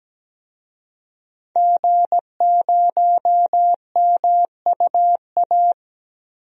Morse code, heard as a keyed tone, about 17 wpm, 700 Hz. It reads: G0MUA
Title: DXCC-100 Call Signs - 5 Characters - Repeated 3x - 17wpm